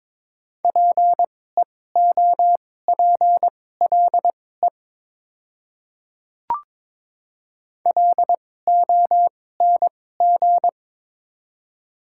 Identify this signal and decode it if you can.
Morse code, heard as a keyed tone, about 22 wpm, spaced as for 16 wpm, 690 Hz. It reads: PEOPLE LONG
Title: 2x Character Spacing - Top 100 Words - 22wpm